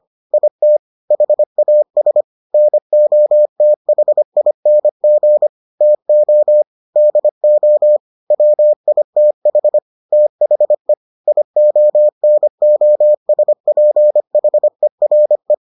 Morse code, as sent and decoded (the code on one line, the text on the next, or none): .. - / .... .- ... / -. --- - .... .. -. --. / - --- / -.. --- / .-- .. - .... / - .... / .. --- -. --- ... .--. .... .-. .
IT HAS NOTHING TO DO WITH THE IONOSPHERE